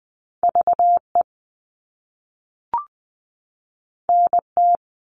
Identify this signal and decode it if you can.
Morse code, heard as a keyed tone, about 20 wpm, 700 Hz. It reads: VE NT
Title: All Cut Numbers - 2-Digit 20wpm